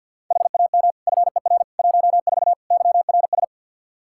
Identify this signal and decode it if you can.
Morse code, heard as a keyed tone, about 46 wpm, 700 Hz. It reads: HAM FER 14 YRS
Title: Practice QSO - Exchange 3 - More Info - 50wpm